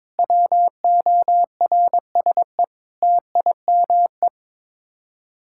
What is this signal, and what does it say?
Morse code, heard as a keyed tone, about 22 wpm, 700 Hz. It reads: WORSE TIME
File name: practice / 22wpm / Sets of 8 Words - 22wpm